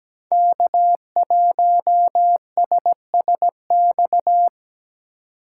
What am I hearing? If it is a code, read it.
Morse code, heard as a keyed tone, about 17 wpm, 700 Hz. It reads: K1SSX
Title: Code Groups - 5 Letter:Numbers - 17wpm